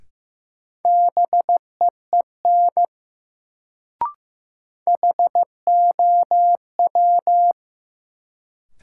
Morse code, heard as a keyed tone, about 15 wpm, 700 Hz. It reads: BEEN HOW
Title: Sentences from Top 100 Words - New Word Drill - 15wpm